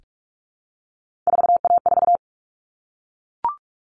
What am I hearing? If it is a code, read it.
Morse code, heard as a keyed tone, about 44 wpm, 700 Hz. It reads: 4A4
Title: Common Cut Numbers - 3-Digit 45wpm